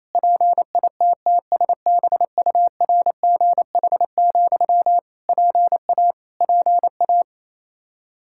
Morse code, heard as a keyed tone, about 28 wpm, 700 Hz. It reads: PITTSBURGH, PA PA
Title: Practice QSO - Exchange 1 - Beginning - 28wpm